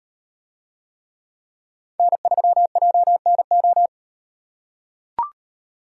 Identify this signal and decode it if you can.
Morse code, heard as a keyed tone, about 36 wpm, 700 Hz. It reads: N3JNO